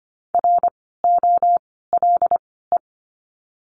Morse code, heard as a keyed tone, about 25 wpm, spaced as for 16 wpm, 700 Hz. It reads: ROLE
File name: course / 25:10wpm / 094 Lesson 16f - Letter c - Words 25wpm